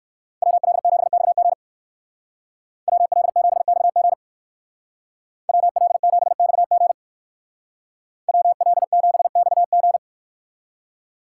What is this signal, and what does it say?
Morse code, heard as a keyed tone, about 59 wpm, 700 Hz. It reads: WL7XG WL7XG WL7XG WL7XG